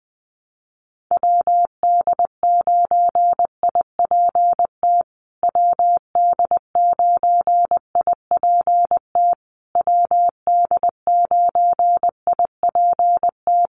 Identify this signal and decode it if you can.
Morse code, heard as a keyed tone, about 20 wpm, 700 Hz. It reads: WD9IPT WD9IPT WD9IPT
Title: USA Call Signs Repeated 3x 20wpm